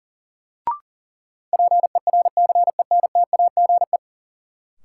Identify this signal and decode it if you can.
Morse code, heard as a keyed tone, about 40 wpm, 700 Hz. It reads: PERCENTAGE
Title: Sentences from Top 1300 Words - New Word Drill Alt - 40wpm